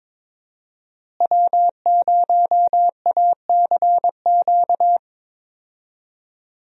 Morse code, heard as a keyed tone, about 22 wpm, 700 Hz. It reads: W0ACQ
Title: USA Call Signs - 5 Characters - Repeated 3x - 22wpm